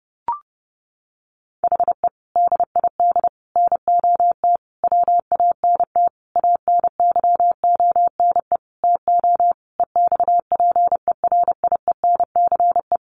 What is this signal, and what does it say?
Morse code, heard as a keyed tone, about 30 wpm, 700 Hz. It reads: HE DID NOT WANT ANYONE TO EXPERIENCE
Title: Sentences from Top 1100 Words 30wpm